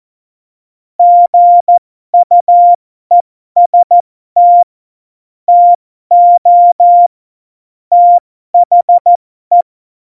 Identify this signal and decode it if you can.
Morse code, heard as a keyed tone, about 14 wpm, spaced as for 12 wpm, 700 Hz. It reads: GUEST TO THE
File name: Morsekurs / Witze